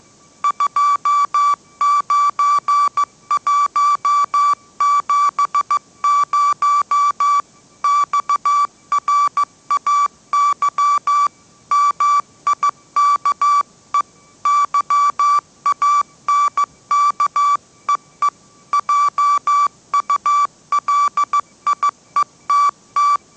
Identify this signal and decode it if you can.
Morse code, heard as a keyed tone, about 15 wpm, 1160 Hz. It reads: 29170 XRAY MIKE YANKEE JULIETT
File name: mrx / mrx066 16wpm